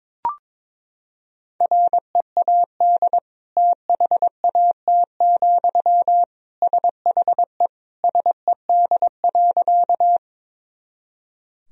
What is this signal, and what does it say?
Morse code, heard as a keyed tone, about 22 wpm, 700 Hz. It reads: READ THAT, SHE SED.
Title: Sentences from Top 200 Words and Abbreviations 22wpm